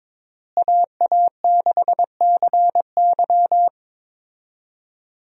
Morse code, heard as a keyed tone, about 22 wpm, 700 Hz. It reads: AA6CY
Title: USA Call Signs - 5 Characters - Repeated 3x - 22wpm